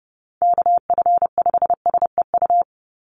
Morse code, heard as a keyed tone, about 30 wpm, 700 Hz. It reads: KF5SEU
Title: USA Call Signs 30wpm